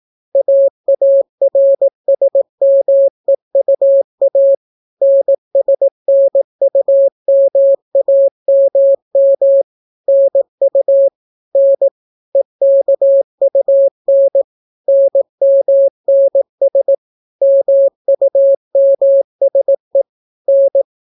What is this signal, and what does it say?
Morse code, heard as a keyed tone, about 18 wpm, 550 Hz. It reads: AARSMEUA NSNUMAMM NU N EKUN NMNS MUMSE N